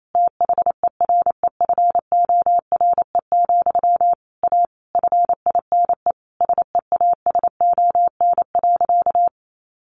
Morse code, heard as a keyed tone, about 28 wpm, 700 Hz. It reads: THEREFORE, A FINE SEASON.
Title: Sentences from Top 1400 Words 28wpm